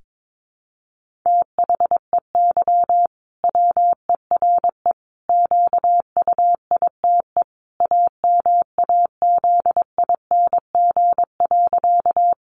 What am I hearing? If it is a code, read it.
Morse code, heard as a keyed tone, about 22 wpm, 700 Hz. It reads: THEY WERE QUITE AMAZING.
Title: Sentences from Top 1500 Words 22wpm